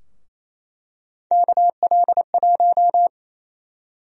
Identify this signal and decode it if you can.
Morse code, heard as a keyed tone, about 28 wpm, 700 Hz. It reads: KL1
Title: USA Call Sign Prefixes - Repeated 2x - 28wpm